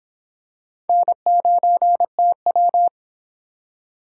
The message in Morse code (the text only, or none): N9TW